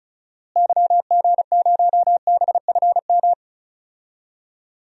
Morse code, heard as a keyed tone, about 36 wpm, 700 Hz. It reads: YG0BFM
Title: DXCC-100 Call Signs - 6 Characters - Repeated 2x - 35wpm